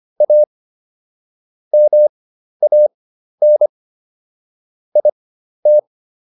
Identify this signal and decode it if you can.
Morse code, heard as a keyed tone, about 25 wpm, spaced as for 12 wpm, 600 Hz. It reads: A MAN IT